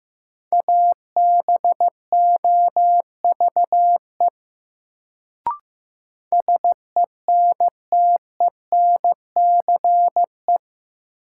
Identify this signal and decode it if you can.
Morse code, heard as a keyed tone, about 15 wpm, 700 Hz. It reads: ABOVE SENTENCE
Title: Sentences from Top 1200 Words - New Word Drill Alt - 15wpm